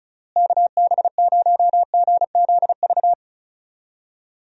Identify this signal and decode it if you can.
Morse code, heard as a keyed tone, about 34 wpm, 700 Hz. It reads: KB0GZV
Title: USA Call Signs - 6 Characters - Repeated 3x - 35wpm